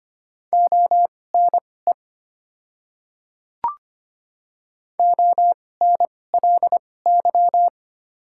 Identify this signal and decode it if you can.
Morse code, heard as a keyed tone, about 25 wpm, spaced as for 18 wpm, 700 Hz. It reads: ONE ONLY
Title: 2x Character Spacing - Top 200 Words - 25wpm